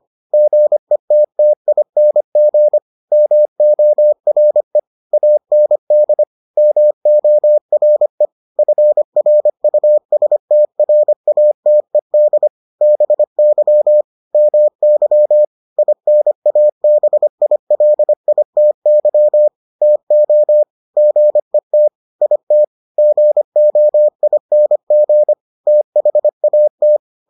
Morse code, heard as a keyed tone, about 25 wpm, 600 Hz. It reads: GETTING MORE AND MORE FRUSTRATED BY MY INABILITY TO GET IT GOING THAT